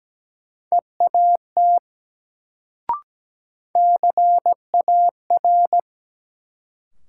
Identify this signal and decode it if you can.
Morse code, heard as a keyed tone, about 17 wpm, 700 Hz. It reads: EAT CAR
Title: Sentences from Top 300 Words - New Word Drill - 17wpm